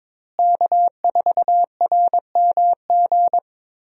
Morse code, heard as a keyed tone, about 22 wpm, 700 Hz. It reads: K4RMG
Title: USA Call Signs - 5 Characters - Repeated 3x - 22wpm